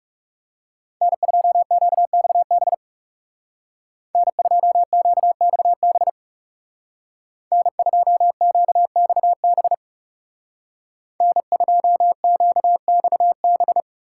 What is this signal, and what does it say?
Morse code, heard as a keyed tone, about 39 wpm, 700 Hz. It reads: N2QXB N2QXB N2QXB N2QXB